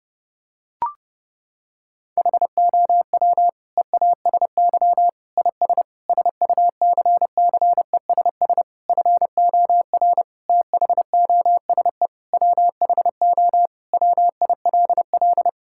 Morse code, heard as a keyed tone, about 30 wpm, 700 Hz. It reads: HOW EASY IS SUCCESS FOR THOSE WHO WILL
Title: Sentences from Top 1000 Words - Review Part 6 - 30wpm